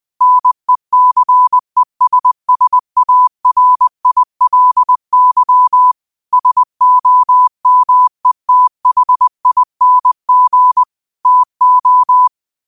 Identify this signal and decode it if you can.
Morse code, heard as a keyed tone, about 20 wpm, 1000 Hz. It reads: NECESSARILY SOMETHING TO
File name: 20wpm